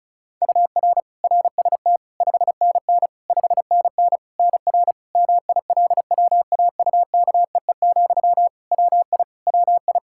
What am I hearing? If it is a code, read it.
Morse code, heard as a keyed tone, about 36 wpm, 700 Hz. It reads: UR RST 5NN 5NN NR MILWAUKEE, WI WI
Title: Practice QSO - Exchange 1 - Beginning - 35wpm